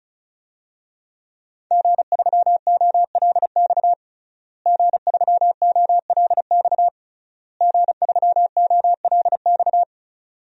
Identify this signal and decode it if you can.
Morse code, heard as a keyed tone, about 34 wpm, 690 Hz. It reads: G3OLX G3OLX G3OLX
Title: DXCC-100 Call Signs - 5 Characters - Repeated 3x - 35wpm